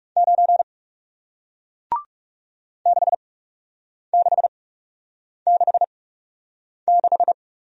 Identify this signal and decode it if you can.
Morse code, heard as a keyed tone, about 40 wpm, spaced as for 8 wpm, 700 Hz. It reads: 9 6666